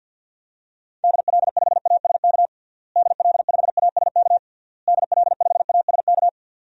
Morse code, heard as a keyed tone, about 49 wpm, 700 Hz. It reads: DL5ASK DL5ASK DL5ASK
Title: DXCC-100 Call Signs - 6 Characters - Repeated 3x - 50wpm